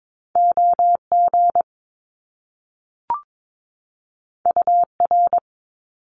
Morse code, heard as a keyed tone, about 22 wpm, 700 Hz. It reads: OG UR
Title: Top Two Letter Combinations in Words 22wpm